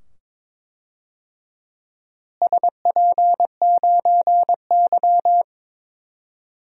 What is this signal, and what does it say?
Morse code, heard as a keyed tone, about 22 wpm, 700 Hz. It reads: SP9Y
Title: DXCC-100 Call Signs - 4 Characters - Repeated 2x - 22wpm